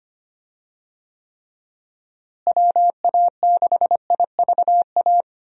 Morse code, heard as a keyed tone, about 25 wpm, 700 Hz. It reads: WA6IVA
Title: USA Call Signs - 6 Characters - Repeated 3x - 25wpm